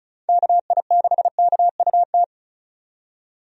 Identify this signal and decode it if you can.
Morse code, heard as a keyed tone, about 34 wpm, 700 Hz. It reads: KI6KUT
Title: USA Call Signs - 6 Characters - 35wpm